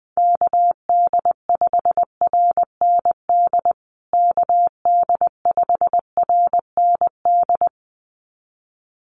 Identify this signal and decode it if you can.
Morse code, heard as a keyed tone, about 20 wpm, 700 Hz. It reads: KD5RND KD5RND
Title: USA Call Signs Repeated 2x 20wpm